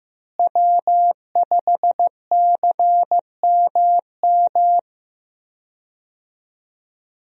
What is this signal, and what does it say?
Morse code, heard as a keyed tone, about 15 wpm, 700 Hz. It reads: W5CMM
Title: USA Call Signs - 5 Characters - Repeated 2x - 15wpm